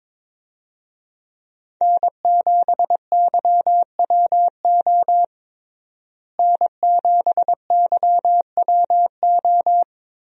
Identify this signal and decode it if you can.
Morse code, heard as a keyed tone, about 22 wpm, 700 Hz. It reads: N7YWO N7YWO
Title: USA Call Signs - 5 Characters - Repeated 3x - 22wpm